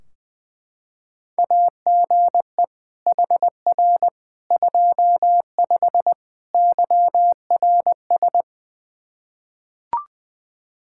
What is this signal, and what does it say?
Morse code, heard as a keyed tone, about 21 wpm, 700 Hz. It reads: AGE HR 25 YRS
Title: K1USN SST QSO Practice 20wpm